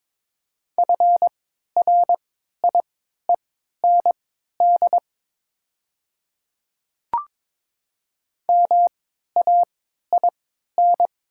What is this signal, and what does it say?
Morse code, heard as a keyed tone, about 22 wpm, spaced as for 13 wpm, 700 Hz. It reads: FRIEND MAIN
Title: 3x Character Spacing - Top 300 Words - 22wpm